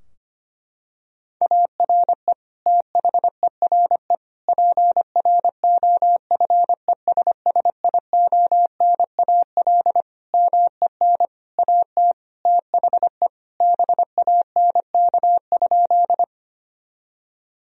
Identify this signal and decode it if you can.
Morse code, heard as a keyed tone, about 25 wpm, 700 Hz. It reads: ARE THERE PROFESSIONAL MEN AT THE BANK?
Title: Sentences from Top 1000 Words - Review Part 4 - 25wpm